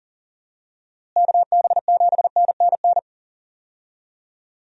Morse code, heard as a keyed tone, about 37 wpm, 700 Hz. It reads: KB7NNN
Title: USA Call Signs - 6 Characters - Repeated 2x - 40wpm